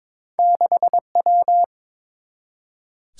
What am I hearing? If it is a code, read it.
Morse code, heard as a keyed tone, about 22 wpm, 700 Hz. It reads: 6W
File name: practice / 22wpm / Two Letter-Number Combinations - 22wpm